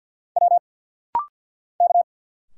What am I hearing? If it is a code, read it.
Morse code, heard as a keyed tone, about 50 wpm, 700 Hz. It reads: W K
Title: Single Letters - Rapid-Fire 50wpm